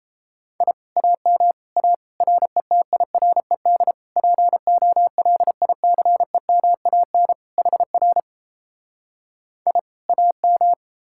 Morse code, heard as a keyed tone, about 37 wpm, 700 Hz. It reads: I AM A RETIRED POLICEMAN HR I AM